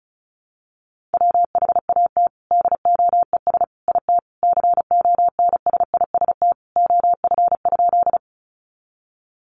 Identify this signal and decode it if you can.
Morse code, heard as a keyed tone, about 34 wpm, 700 Hz. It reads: WHAT DOES IT CONSIST OF?
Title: Sentences from Top 1400 Words 35wpm